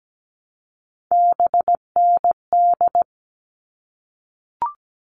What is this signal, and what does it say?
Morse code, heard as a keyed tone, about 17 wpm, 700 Hz. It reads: BND
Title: All Cut Numbers - 3-Digit 17wpm